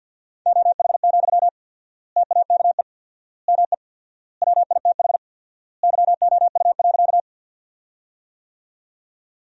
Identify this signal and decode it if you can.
Morse code, heard as a keyed tone, about 53 wpm, 700 Hz. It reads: OH, TAKE ME WITH YOU.
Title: 4x Word-Spacing - Sentences from Top 500 Words - 50wpm